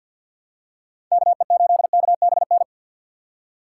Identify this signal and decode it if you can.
Morse code, heard as a keyed tone, about 53 wpm, 700 Hz. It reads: KE8KBN
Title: USA Call Signs - 4x Character Spacing - 50wpm